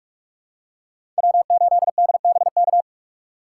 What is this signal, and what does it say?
Morse code, heard as a keyed tone, about 46 wpm, 700 Hz. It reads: W8DBK
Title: USA Call Signs - 5 Characters - 2x Character Spacing - 45wpm